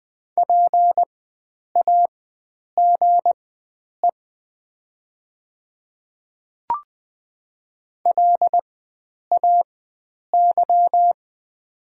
Morse code, heard as a keyed tone, about 20 wpm, spaced as for 9 wpm, 700 Hz. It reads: PAGE LAY
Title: 4x Character Spacing - Top 500 Words - 20wpm